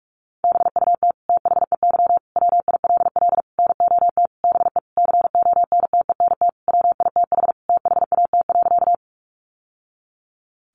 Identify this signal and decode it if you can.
Morse code, heard as a keyed tone, about 44 wpm, 700 Hz. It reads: BUT THEY WILL NOT BE CONTENT WITH THAT.
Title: Sentences from Top 900 Words 45wpm